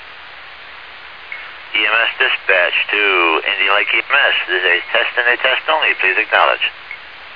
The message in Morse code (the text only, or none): none